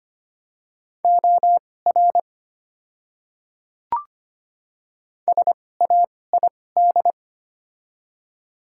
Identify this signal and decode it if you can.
Morse code, heard as a keyed tone, about 25 wpm, spaced as for 18 wpm, 700 Hz. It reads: OR SAID
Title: 2x Character Spacing - Top 300 Words - 25wpm